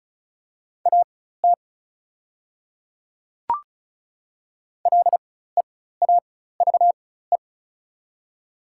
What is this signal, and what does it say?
Morse code, heard as a keyed tone, about 34 wpm, spaced as for 16 wpm, 700 Hz. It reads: AT LEAVE
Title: 4x Character Spacing - Top 500 Words - 35wpm